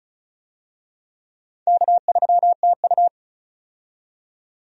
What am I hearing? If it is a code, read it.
Morse code, heard as a keyed tone, about 36 wpm, 700 Hz. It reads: K3TU
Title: USA Call Signs - 4 Characters - 35wpm